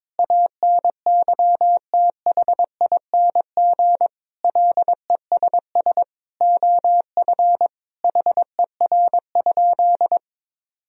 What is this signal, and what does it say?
Morse code, heard as a keyed tone, about 22 wpm, 700 Hz. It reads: ANYTHING LESS OF HER?
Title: Sentences from Top 1000 Words - Review Part 2 - 22wpm